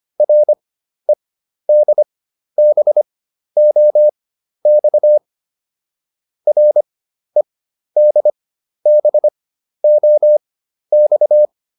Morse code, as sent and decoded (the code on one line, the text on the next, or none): .-. -.. -... --- -..- / .-. -.. -... --- -..-
REDBOX REDBOX